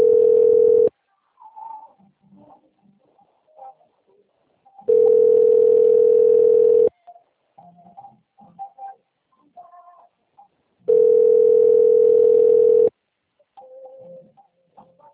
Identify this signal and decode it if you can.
Ringback tone (North America), 440 + 480 Hz.
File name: out- - - - - .14719